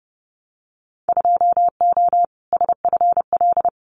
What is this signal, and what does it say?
Morse code, heard as a keyed tone, about 30 wpm, 700 Hz. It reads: 2O SFL
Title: Winter Field Day Exchanges 30wpm